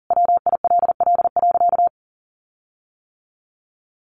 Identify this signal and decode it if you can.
Morse code, heard as a keyed tone, about 40 wpm, 700 Hz. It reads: WILL.
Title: Sentences from Top 1400 Words 40wpm